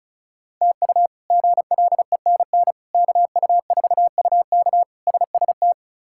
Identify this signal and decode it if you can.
Morse code, heard as a keyed tone, about 34 wpm, 700 Hz. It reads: TU GLENN KU4UK SST
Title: K1USN SST QSO Practice - 35wpm